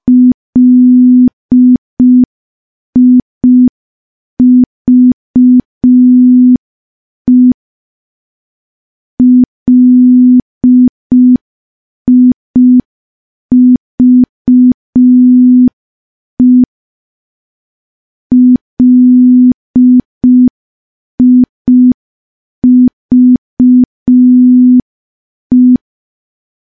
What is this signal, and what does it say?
Morse code, heard as a keyed tone, about 5 wpm, 250 Hz. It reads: LIVE LIVE LIVE
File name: morse / morse-training-05WPM